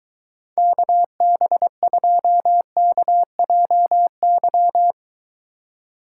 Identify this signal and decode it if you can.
Morse code, heard as a keyed tone, about 23 wpm, 700 Hz. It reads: KB2KJY